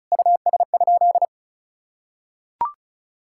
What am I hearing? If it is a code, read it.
Morse code, heard as a keyed tone, about 34 wpm, 700 Hz. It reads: US?
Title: Sentences from Top 1000 Words - Review Part 6 - 35wpm